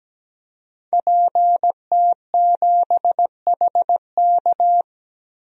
Morse code, heard as a keyed tone, about 17 wpm, 700 Hz. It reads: PT7HK
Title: DXCC-100 Call Signs - 5 Characters - 17wpm